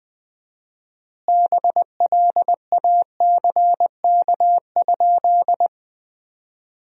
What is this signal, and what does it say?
Morse code, heard as a keyed tone, about 20 wpm, 700 Hz. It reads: BLACK?
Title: 4x Word-Spacing - Sentences from Top 400 Words - 20wpm